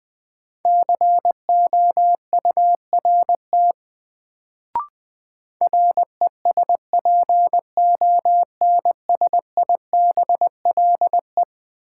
Morse code, heard as a keyed tone, about 20 wpm, 700 Hz. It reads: COURT RESPONSIBLE
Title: Sentences from Top 1600 Words - New Word Drill Alt - 20wpm